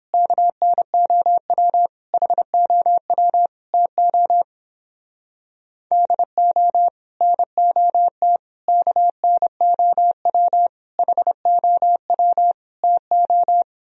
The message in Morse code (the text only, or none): KNOW HOW TO DO NOT KNOW HOW TO